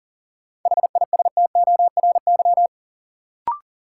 Morse code, heard as a keyed tone, about 39 wpm, 700 Hz. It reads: HISTORY
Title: Sentences from Top 600 Words - New Word Drill - 40wpm